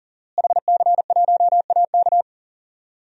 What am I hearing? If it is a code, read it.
Morse code, heard as a keyed tone, about 37 wpm, 700 Hz. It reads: HC1AK
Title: DXCC-100 Call Signs - 5 Characters - Repeated 2x - 40wpm